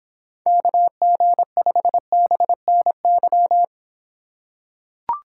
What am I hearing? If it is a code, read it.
Morse code, heard as a keyed tone, about 25 wpm, 700 Hz. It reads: KG5BNY